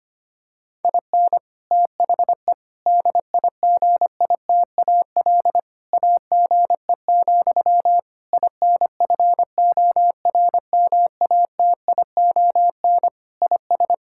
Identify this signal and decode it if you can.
Morse code, heard as a keyed tone, about 25 wpm, 700 Hz. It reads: IN THE DIGITAL AGE, INFORMATION IS
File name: practice / 25wpm / Difficult Sentences with Hints - 25wpm